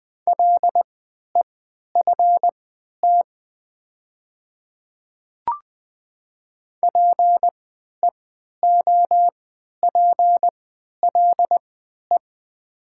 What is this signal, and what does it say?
Morse code, heard as a keyed tone, about 20 wpm, spaced as for 11 wpm, 700 Hz. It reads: LEFT PEOPLE
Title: 3x Character Spacing - Top 300 Words - 20wpm